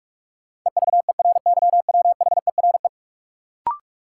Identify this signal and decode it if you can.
Morse code, heard as a keyed tone, about 42 wpm, 700 Hz. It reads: EVERYWHERE
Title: Sentences from Top 1600 Words - New Word Drill - 45wpm